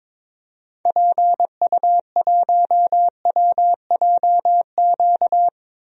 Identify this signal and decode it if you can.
Morse code, heard as a keyed tone, about 22 wpm, 700 Hz. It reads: PU1WJQ
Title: DXCC-100 Call Signs - 6 Characters - 22wpm